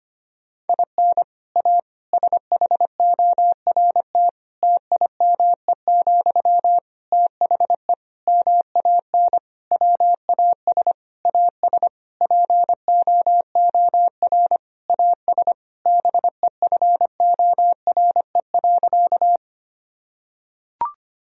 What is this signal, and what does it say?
Morse code, heard as a keyed tone, about 25 wpm, 700 Hz. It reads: IN A SHORT TIME, THE MAN WAS AS POOR AS BEFORE.
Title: Sentences from Top 1000 Words - Review Part 2 - 25wpm